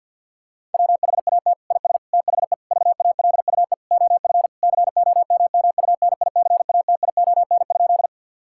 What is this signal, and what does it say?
Morse code, heard as a keyed tone, about 49 wpm, 700 Hz. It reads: WHAT IS THE VALUE OF COMMUNICATION?
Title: Sentences from Top 1000 Words - Review Part 6 - 50wpm